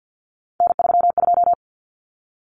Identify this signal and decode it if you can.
Morse code, heard as a keyed tone, about 48 wpm, 700 Hz. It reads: N32
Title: Common Cut Numbers - 3-Digit 50wpm